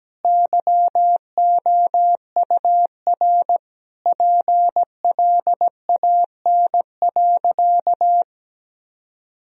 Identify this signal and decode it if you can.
Morse code, heard as a keyed tone, about 17 wpm, 700 Hz. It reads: YOUR PLAN.
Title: Sentences from Top 1000 Words - Review Part 2 - 17wpm